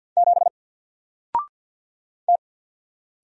Morse code, heard as a keyed tone, about 50 wpm, 700 Hz. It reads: 7 T
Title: Single Letter-Number 50wpm